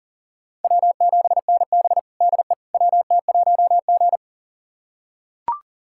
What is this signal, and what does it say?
Morse code, heard as a keyed tone, about 40 wpm, 700 Hz. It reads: W7NB DE WT1G
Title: USA Callsign DE Callsign - 4 Characters - 40wpm